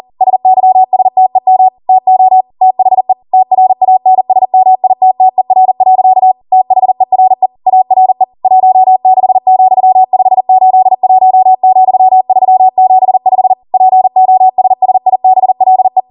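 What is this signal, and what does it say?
Morse code, heard as a keyed tone, about 40 wpm, 750 Hz. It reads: SYSTEM TO THE TRANSMITTER. THERE ARE 16,581,375 POSSIBLE
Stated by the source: Code practice at 40 wpm.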